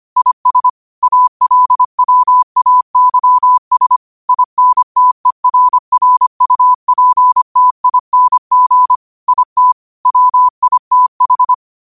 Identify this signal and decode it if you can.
Morse code, heard as a keyed tone, about 25 wpm, 1000 Hz. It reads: IS ALWAYS INTERRUPTING IT WITH